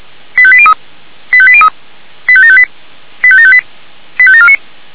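Five-tone selective call (ZVEI), heard as quick runs of five tones, tones of about 70 ms. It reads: 85803 85803 85758 85758 85749
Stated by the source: ZVEI five-tone selective call received by SDR.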